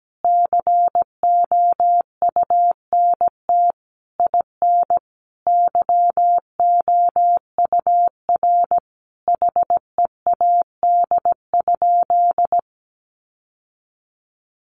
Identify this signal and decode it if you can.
Morse code, heard as a keyed tone, about 17 wpm, 700 Hz. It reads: COUNT IN YOUR HEAD?
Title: Sentences from Top 1500 Words 17wpm